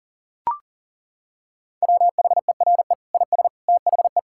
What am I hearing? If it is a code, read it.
Morse code, heard as a keyed tone, about 37 wpm, 700 Hz. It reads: WHERE IS THE
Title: Sentences from Top 1600 Words 40wpm